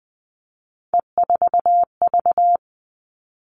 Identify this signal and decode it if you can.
Morse code, heard as a keyed tone, about 20 wpm, 700 Hz. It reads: E4V
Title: All Cut Numbers - 3-Digit 20wpm